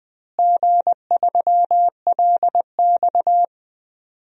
Morse code, heard as a keyed tone, about 20 wpm, 700 Hz. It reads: G3LX
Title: DXCC-100 Call Signs - 4 Characters - Repeated 3x - 20wpm